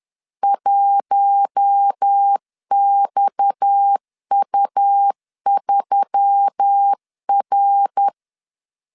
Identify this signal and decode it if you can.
Morse code, heard as a keyed tone, about 10 wpm, 800 Hz. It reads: 1XU3R
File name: LEZIONI / 6 PKX0 / 6 PKX 10wpm 2426